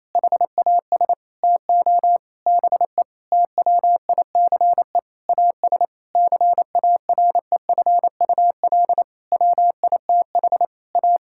Morse code, heard as a keyed tone, about 27 wpm, 700 Hz. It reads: HAS TO BE TWICE AS CAREFUL WITH A
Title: Sentences from Top 1600 Words 28wpm